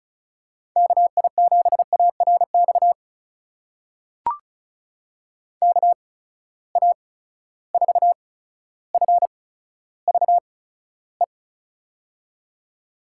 Morse code, heard as a keyed tone, about 36 wpm, spaced as for 15 wpm, 700 Hz. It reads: KI7ARX KA4FVE
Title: USA Call Signs - 6 Characters - 8x Character Spacing - 35wpm